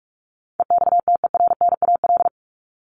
Morse code, heard as a keyed tone, about 44 wpm, 700 Hz. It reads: EXTERNAL
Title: Top 1751-2000 Words 45wpm